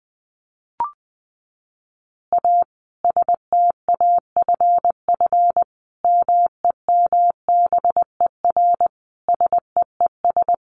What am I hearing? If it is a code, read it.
Morse code, heard as a keyed tone, about 19 wpm, 700 Hz. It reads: A STAFF MEMBER SEES
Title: Additional Easy Phrases 20wpm